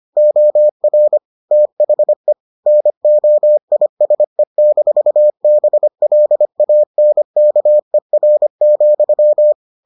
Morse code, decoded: OR THE NOISE-BLANKER,

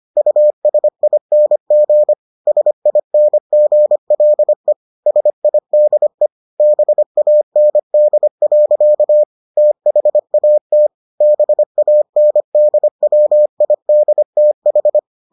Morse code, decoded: USING SINGLE SIDE BAND. THAT BANDWIDTH

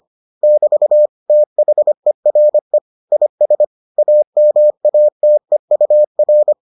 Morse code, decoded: <BT> THERE IS AMATEUR

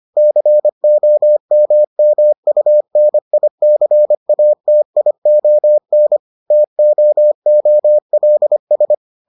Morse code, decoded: COMMUNICATION TOOLS

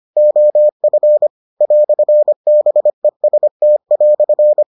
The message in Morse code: OF "BEST"

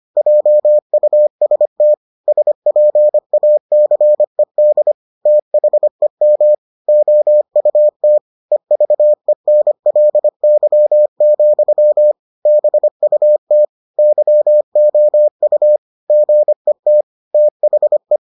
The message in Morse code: JUST SPACED THEM OUT EVENLY, BUT YOU GET THE

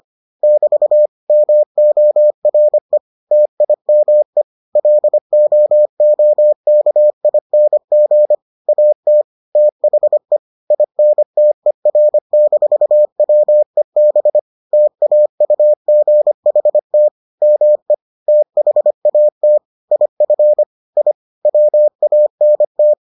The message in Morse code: <BT> MORE TIME LOOKING AT THE INTER-WEB TAUGHT ME THAT IF I WANT

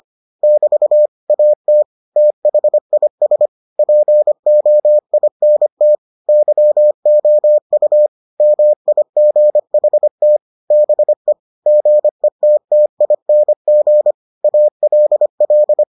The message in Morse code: <BT> AT THIS POINT YOU MIGHT BE GETTING ALL

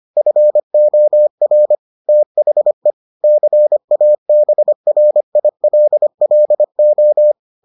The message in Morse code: FOR THE CABRILLO

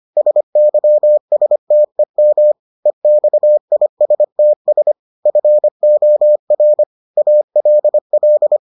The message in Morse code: SYSTEM EXISTS FOR ALL

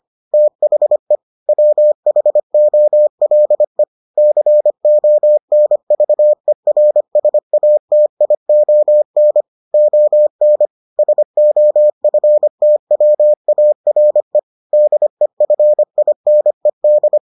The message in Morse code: THE WHOLE CONVERSATION ON SOFTWARE DEFINED

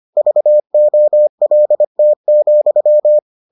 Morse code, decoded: VOLT,

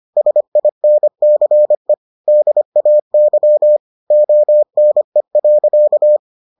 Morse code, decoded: SINCE DAY ONE.

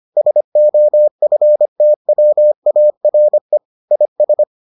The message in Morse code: SOFTWARE IS

